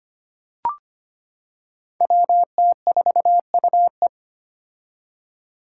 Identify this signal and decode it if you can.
Morse code, heard as a keyed tone, about 25 wpm, 700 Hz. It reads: WT4UE
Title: Five Challenging Letter-Number Combinations 25wpm